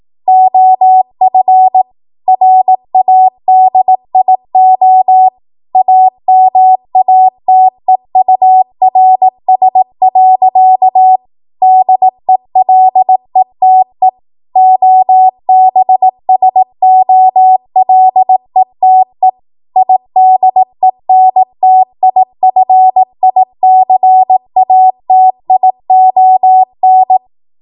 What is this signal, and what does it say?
Morse code, heard as a keyed tone, about 18 wpm, 750 Hz. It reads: OF RADIO AMATEURS. DELETE OBSOLETE IDENTIFICATION
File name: Morse / 18WPM